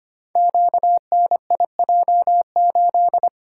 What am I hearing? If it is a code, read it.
Morse code, heard as a keyed tone, about 25 wpm, 700 Hz. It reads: QNIJ8
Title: Code Groups - 5 Letter:Numbers - 25wpm